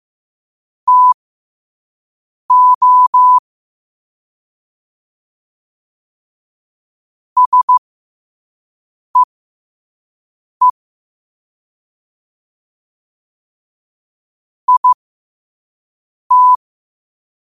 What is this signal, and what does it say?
Morse code, heard as a keyed tone, about 15 wpm, spaced as for 5 wpm, 1000 Hz. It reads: TO SEE IT